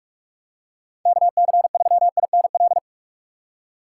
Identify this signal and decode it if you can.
Morse code, heard as a keyed tone, about 42 wpm, 700 Hz. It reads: KC3INL
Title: USA Call Signs - 4x Character Spacing - 45wpm